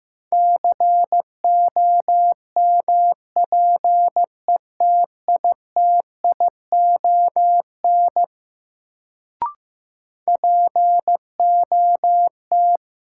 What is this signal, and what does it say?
Morse code, heard as a keyed tone, about 15 wpm, 700 Hz. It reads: COMPETITION POT
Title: Sentences from Top 1000 Words - New Word Drill - 15wpm